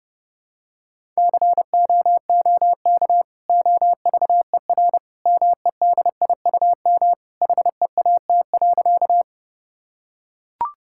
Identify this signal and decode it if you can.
Morse code, heard as a keyed tone, about 32 wpm, 700 Hz. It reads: COOK OVER MEDIUM HEAT.
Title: Sentences from Top 1000 Words - Review Part 6 - 30wpm